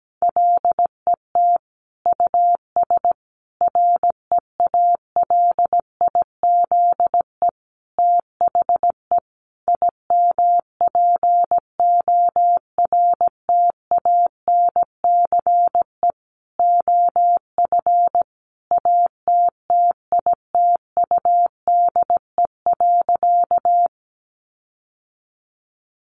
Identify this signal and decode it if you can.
Morse code, heard as a keyed tone, about 17 wpm, 700 Hz. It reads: LET US REALIZE THE IMPORTANCE OF ATTITUDE.
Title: Sentences from Top 1300 Words 17wpm